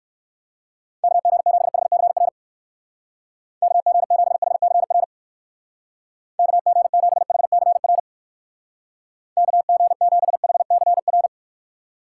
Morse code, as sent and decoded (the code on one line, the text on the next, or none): -.- --. --... .... -.-. .-. / -.- --. --... .... -.-. .-. / -.- --. --... .... -.-. .-. / -.- --. --... .... -.-. .-.
KG7HCR KG7HCR KG7HCR KG7HCR